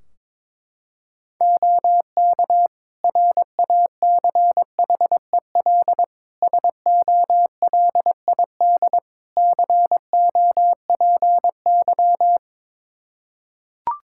Morse code, heard as a keyed tone, about 22 wpm, 700 Hz. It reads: OK RACHEL SOLID COPY
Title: K1USN SST QSO Practice 22wpm